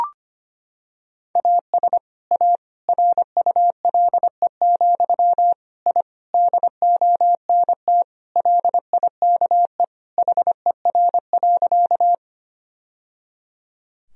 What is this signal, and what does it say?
Morse code, heard as a keyed tone, about 25 wpm, 700 Hz. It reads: AS A RULE, I DONT LIKE HER.
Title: Sentences from Top 1000 Words - Review Part 2 - 25wpm